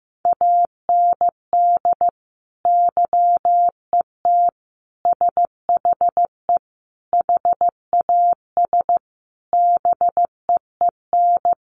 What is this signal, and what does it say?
Morse code, heard as a keyed tone, about 15 wpm, 700 Hz. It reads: AND YET SHE HAS BEEN
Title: Sentences from Top 1100 Words 15wpm